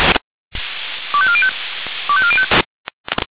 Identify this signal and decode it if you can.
Five-tone selective call (ZVEI), heard as quick runs of five tones, tones of about 70 ms. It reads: 25405 25405